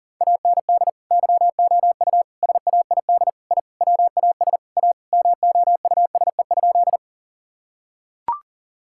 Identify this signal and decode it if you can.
Morse code, heard as a keyed tone, about 40 wpm, 700 Hz. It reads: AND YOU SAID I WAS A MOUSE?
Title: Sentences from Top 1000 Words - Review Part 5 - 40wpm